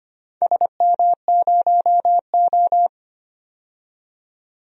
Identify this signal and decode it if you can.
Morse code, heard as a keyed tone, about 25 wpm, 690 Hz. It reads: SM0O
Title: DXCC-100 Call Signs - 4 Characters - 25wpm